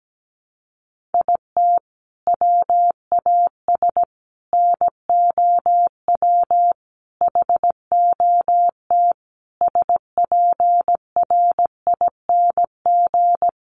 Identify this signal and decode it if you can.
Morse code, heard as a keyed tone, about 17 wpm, 700 Hz. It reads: IT WAS NOW HOT SPRING
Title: Sentences from Top 1200 Words 17wpm